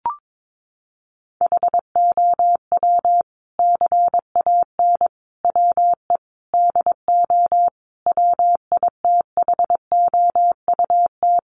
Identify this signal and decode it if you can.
Morse code, heard as a keyed tone, about 22 wpm, 700 Hz. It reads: HOW CAN WE DO WITHOUT
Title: Sentences from Top 600 Words 22wpm